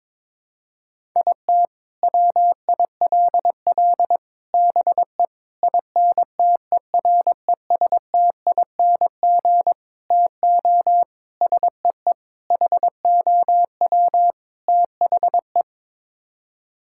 Morse code, heard as a keyed tone, about 22 wpm, 700 Hz. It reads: IT WILL BE INTERESTING TO SEE HOW THE
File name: practice / 22wpm / Sets of 8 Words - 22wpm